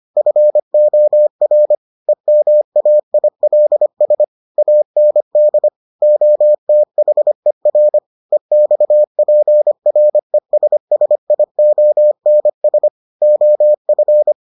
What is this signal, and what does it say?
Morse code, heard as a keyed tone, about 25 wpm, 600 Hz. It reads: FOR EMAILS AND OTHER EXPRESSIONS OF